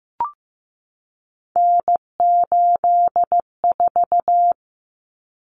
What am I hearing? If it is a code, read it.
Morse code, heard as a keyed tone, about 15 wpm, 700 Hz. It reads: N84
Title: Common Cut Numbers - 3-Digit 15wpm